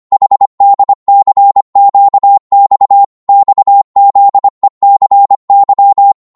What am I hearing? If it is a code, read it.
Morse code, heard as a keyed tone, about 25 wpm, 800 Hz. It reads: HDCQX XZECY
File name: Morse / ltr-25wpm